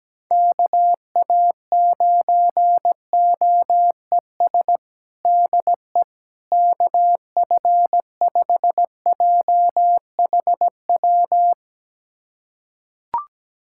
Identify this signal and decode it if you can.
Morse code, heard as a keyed tone, about 17 wpm, 700 Hz. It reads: KA9OES DE KF5JHW
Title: USA Callsign DE Callsign 17wpm